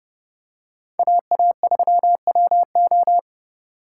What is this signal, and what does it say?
Morse code, heard as a keyed tone, about 29 wpm, 700 Hz. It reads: AA3WO